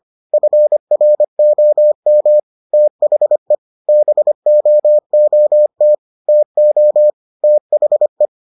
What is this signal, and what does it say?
Morse code, heard as a keyed tone, about 25 wpm, 600 Hz. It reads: FROM THE BOOT TO THE